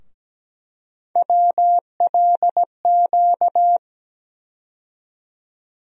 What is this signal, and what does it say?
Morse code, heard as a keyed tone, about 17 wpm, 700 Hz. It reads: WLQ
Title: USA Call Sign Postfixes - 3 Letters 17wpm